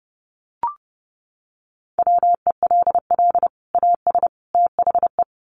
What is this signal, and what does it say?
Morse code, heard as a keyed tone, about 30 wpm, 700 Hz. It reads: WELL AS THE
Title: Sets of 3 Words 30wpm